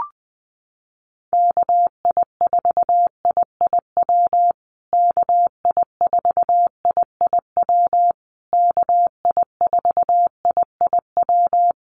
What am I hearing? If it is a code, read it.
Morse code, heard as a keyed tone, about 20 wpm, 700 Hz. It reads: KI4IIW KI4IIW KI4IIW
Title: USA Call Signs Repeated 3x 20wpm